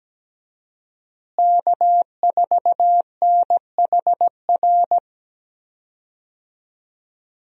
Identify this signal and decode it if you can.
Morse code, heard as a keyed tone, about 17 wpm, 700 Hz. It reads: K4NHR
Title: USA Call Signs - 5 Characters - Repeated 3x - 17wpm